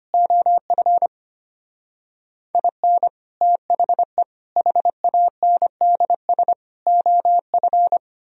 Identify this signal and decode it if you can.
Morse code, heard as a keyed tone, about 31 wpm, 700 Hz. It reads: OF IN THE HANDS OF
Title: Speed-Racing - Sets of 4 Words 25wpm